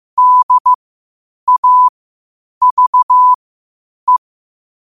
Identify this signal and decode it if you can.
Morse code, heard as a keyed tone, about 15 wpm, spaced as for 8 wpm, 1000 Hz. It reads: DAVE
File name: qotdfiles / 8wpm